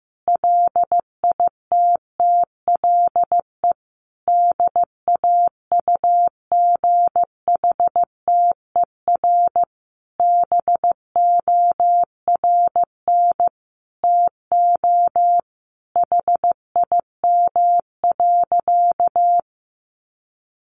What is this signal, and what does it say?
Morse code, heard as a keyed tone, about 15 wpm, 700 Hz. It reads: LITTLE DAUGHTER BORN TO HIM.
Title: Sentences from Top 1500 Words 15wpm